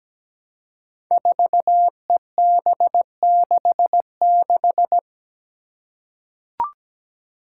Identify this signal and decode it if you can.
Morse code, heard as a keyed tone, about 17 wpm, 700 Hz. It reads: 4EB66
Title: Five Challenging Letter-Number Combinations 17wpm